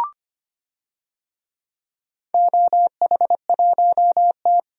Morse code, heard as a keyed tone, about 25 wpm, 700 Hz. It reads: OH1T
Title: DXCC-100 Call Signs - 4 Characters - 25wpm